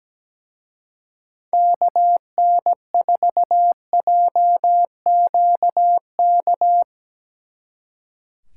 Morse code, heard as a keyed tone, about 19 wpm, 700 Hz. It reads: KN4JQK
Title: Speed Racing - USA 2x3 Call Signs 17wpm